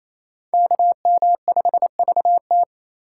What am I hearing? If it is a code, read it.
Morse code, heard as a keyed tone, about 37 wpm, 700 Hz. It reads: KM5VT